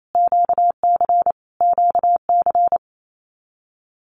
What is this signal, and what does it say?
Morse code, heard as a keyed tone, about 28 wpm, 700 Hz. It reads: QC QC
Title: Canadian Provinces and Territories - Repeated 3x - 28wpm